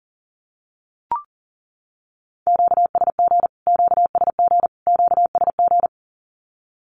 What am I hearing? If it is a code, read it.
Morse code, heard as a keyed tone, about 40 wpm, 700 Hz. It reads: QSG QSG QSG
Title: Q-Codes Repeated 3x 40wpm